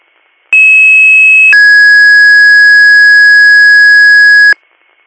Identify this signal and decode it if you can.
Two-tone page, heard as a one-second tone, then a three-second tone, about 2600 then 1650 Hz. It is heard from 0.5 to 4.5 s.